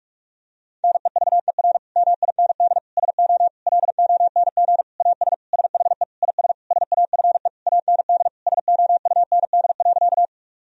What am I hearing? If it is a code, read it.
Morse code, heard as a keyed tone, about 42 wpm, 700 Hz. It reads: NEVER MIND SO LONG AS SHE IS SAFE AND SOUND.
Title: Sentences from Top 1000 Words - Review Part 1 - 45wpm